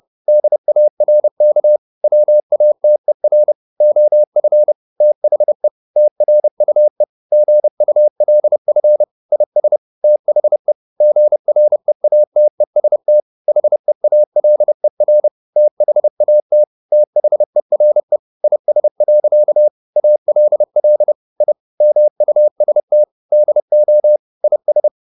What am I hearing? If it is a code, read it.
Morse code, heard as a keyed tone, about 30 wpm, 600 Hz. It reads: DARK WATER OF THE TRUE GULF IS THE GREATEST HEALER THAT THERE IS. ALL I MUST DO IS